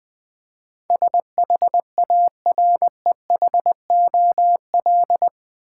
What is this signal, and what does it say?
Morse code, heard as a keyed tone, about 20 wpm, 700 Hz. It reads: SHAREHOL
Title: Build A Word - 4 Letters At A Time - 11-Letter Words - 20wpm